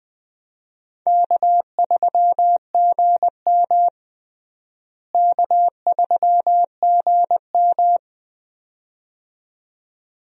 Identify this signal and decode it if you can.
Morse code, heard as a keyed tone, about 20 wpm, 700 Hz. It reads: K3GM K3GM
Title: USA Call Signs - 4 Characters - Repeated 2x - 20wpm